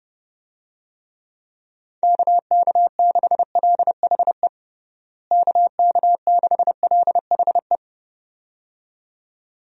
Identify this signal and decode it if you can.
Morse code, heard as a keyed tone, about 30 wpm, 700 Hz. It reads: KK6LHE KK6LHE
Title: USA Call Signs - 6 Characters - Repeated 2x - 30wpm